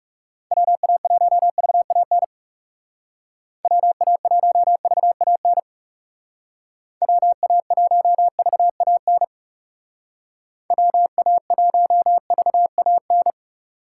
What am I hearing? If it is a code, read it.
Morse code, heard as a keyed tone, about 38 wpm, 700 Hz. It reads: WA1VAN WA1VAN WA1VAN WA1VAN